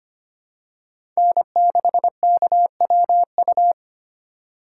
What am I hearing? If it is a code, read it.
Morse code, heard as a keyed tone, about 25 wpm, 700 Hz. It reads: N6KWU
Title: USA Call Signs - 5 Characters - 2x Character Spacing - 25wpm